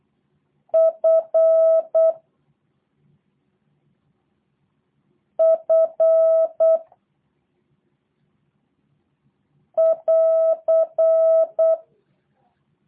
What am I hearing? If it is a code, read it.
Morse code, heard as a keyed tone, about 8 wpm, 650 Hz. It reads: F F <AR>
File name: jh8cbh / 2023cw / review1-4